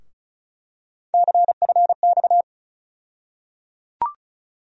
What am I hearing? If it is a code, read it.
Morse code, heard as a keyed tone, about 36 wpm, 700 Hz. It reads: CFX
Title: USA Call Sign Postfixes - 3 Letters 35wpm